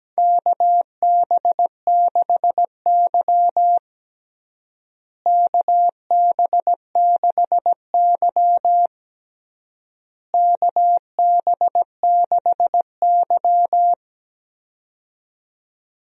Morse code, heard as a keyed tone, about 17 wpm, 700 Hz. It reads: KB6Y KB6Y KB6Y
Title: USA Call Signs - 4 Characters - Repeated 3x - 17wpm